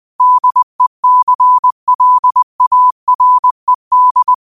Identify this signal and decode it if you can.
Morse code, heard as a keyed tone, about 20 wpm, 1000 Hz. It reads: DECLARED
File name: qotdfiles / 20wpm